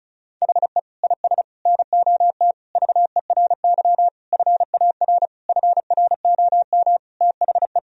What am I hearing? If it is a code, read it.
Morse code, heard as a keyed tone, about 36 wpm, 700 Hz. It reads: HE IS NOT VERY FAR FROM THE
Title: Sentences from Top 1000 Words - Review Part 1 - 35wpm